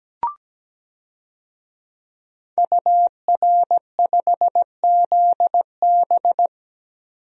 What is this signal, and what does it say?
Morse code, heard as a keyed tone, about 17 wpm, 700 Hz. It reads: UR5ZB
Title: DXCC-100 Call Signs - 5 Characters - 17wpm